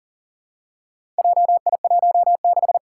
Morse code, heard as a keyed tone, about 40 wpm, 700 Hz. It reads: JI16
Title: Four Challenging Letter-Number Combinations 40wpm